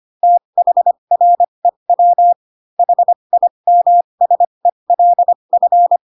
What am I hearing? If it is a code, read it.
Morse code, heard as a keyed tone, about 25 wpm, spaced as for 22 wpm, 700 Hz. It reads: THREW HIMSELF
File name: Pinocchio / Pinocchio ch2 18 25 ik1hge